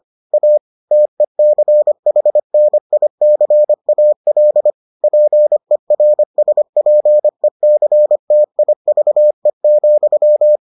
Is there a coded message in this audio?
Yes: Morse code from 0.3 to 10.6 s, heard as a keyed tone, about 25 wpm, 600 Hz. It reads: A TECHNICAL PERSPECTIVE,